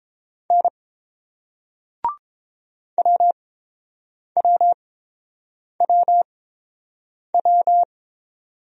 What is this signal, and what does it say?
Morse code, heard as a keyed tone, about 29 wpm, spaced as for 7 wpm, 700 Hz. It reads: N WWWW